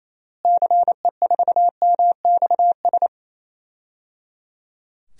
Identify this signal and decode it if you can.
Morse code, heard as a keyed tone, about 28 wpm, 700 Hz. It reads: CE4MXS
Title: DXCC-100 Call Signs - 6 Characters - Repeated 3x - 28wpm